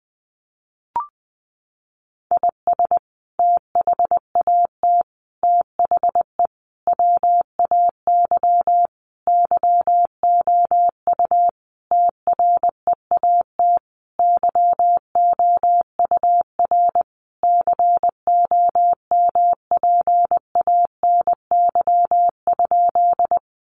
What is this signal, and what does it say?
Morse code, heard as a keyed tone, about 20 wpm, 700 Hz. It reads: IS THAT THE WAY YOU TREAT YOUR COMPANY?
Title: Sentences from Top 1300 Words 20wpm